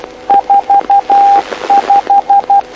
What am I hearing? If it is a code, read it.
Morse code, heard as a keyed tone, about 12 wpm, 800 Hz. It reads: 45
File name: lu4aao / lu7aa 09dic2011 exp planeador vuelo3